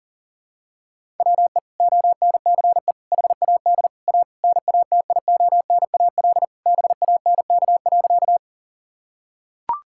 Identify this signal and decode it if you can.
Morse code, heard as a keyed tone, about 40 wpm, 700 Hz. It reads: WE ONCE HAD A NATIONAL BANK.
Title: Sentences from Top 1000 Words - Review Part 6 - 40wpm